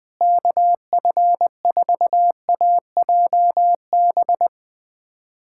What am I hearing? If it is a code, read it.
Morse code, heard as a keyed tone, about 20 wpm, 700 Hz. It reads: KF4AJB
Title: USA Callsign DE Callsign 20wpm